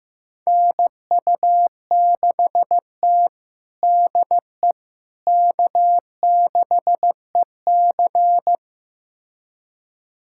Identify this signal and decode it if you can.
Morse code, heard as a keyed tone, about 15 wpm, 700 Hz. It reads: NU6T DE K6EC
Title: USA Callsign DE Callsign - 4 Characters - 15wpm